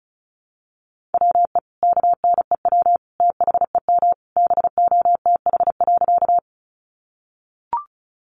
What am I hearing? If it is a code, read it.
Morse code, heard as a keyed tone, about 34 wpm, 700 Hz. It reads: WE KNEW THEM BOTH.
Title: Sentences from Top 400 Words 35wpm